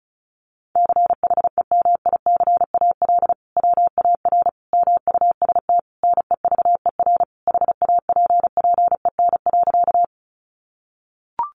Morse code, heard as a keyed tone, about 34 wpm, 700 Hz. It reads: CHEMICAL WAR MUST NEVER HAPPEN.
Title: Sentences from Top 1200 Words 35wpm